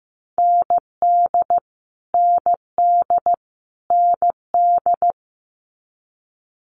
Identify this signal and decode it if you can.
Morse code, heard as a keyed tone, about 15 wpm, 700 Hz. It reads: ND ND ND
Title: Multi-letter Phonemes Repeated 3x 15wpm